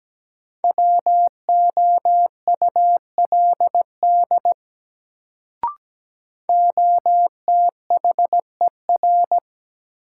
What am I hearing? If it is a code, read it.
Morse code, heard as a keyed tone, about 17 wpm, 700 Hz. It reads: WOULD OTHER
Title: Sentences from Top 100 Words - New Word Drill - 17wpm